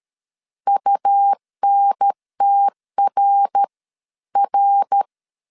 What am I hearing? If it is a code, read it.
Morse code, heard as a keyed tone, about 13 wpm, spaced as for 11 wpm, 800 Hz. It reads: UNTRR
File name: CW / DET / 1 3 1500C S B 12wpm 2828